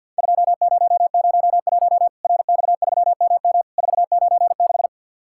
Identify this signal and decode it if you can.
Morse code, heard as a keyed tone, about 50 wpm, 700 Hz. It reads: 2001 RX3MM 496
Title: Worked All Europe DX Contest - QTC Exchange - 50wpm